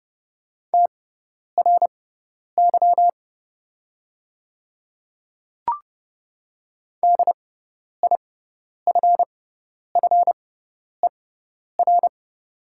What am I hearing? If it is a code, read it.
Morse code, heard as a keyed tone, about 30 wpm, spaced as for 10 wpm, 700 Hz. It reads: TRY DIFFER
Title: 6x Character Spacing - Top 200 Words - 30wpm